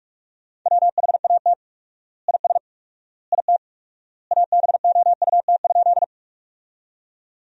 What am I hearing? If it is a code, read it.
Morse code, heard as a keyed tone, about 41 wpm, 700 Hz. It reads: WHAT IS IT ABOUT?
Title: 4x Word-Spacing - Sentences from Top 100 Words - 45wpm